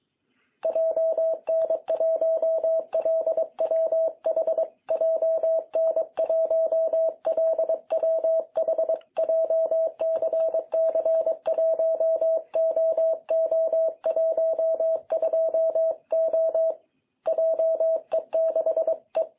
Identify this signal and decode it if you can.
Morse code, heard as a keyed tone, about 22 wpm, 660 Hz. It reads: JN1LWH JN1LWH JCC1OO12O JE6E
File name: jh8cbh / 2021cw